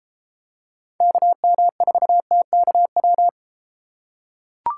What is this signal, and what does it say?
Morse code, heard as a keyed tone, about 31 wpm, 700 Hz. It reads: KM4TKW